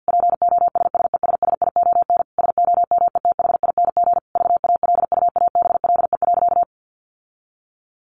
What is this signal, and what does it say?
Morse code, heard as a keyed tone, about 50 wpm, 700 Hz. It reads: POSSESSION SOMETHING VALUABLE.
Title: Sentences from Top 1400 Words 50wpm